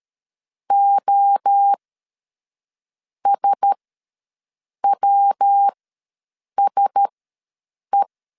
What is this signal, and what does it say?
Morse code, heard as a keyed tone, about 13 wpm, spaced as for 6 wpm, 800 Hz. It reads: OSWSE
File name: LEZIONI / 5 LBVC / 5 LB ws 12wpm 3034